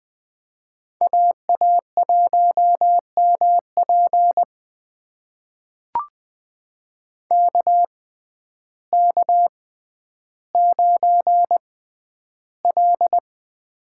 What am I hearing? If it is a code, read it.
Morse code, heard as a keyed tone, about 20 wpm, spaced as for 11 wpm, 700 Hz. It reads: AA1MP KK9L
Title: USA Call Signs - 6x Character Spacing - 20wpm